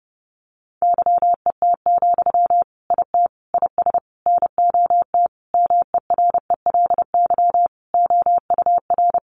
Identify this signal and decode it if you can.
Morse code, heard as a keyed tone, about 30 wpm, 700 Hz. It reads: YET, IT IS NOT MERELY OUR
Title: Sentences from Top 1400 Words 30wpm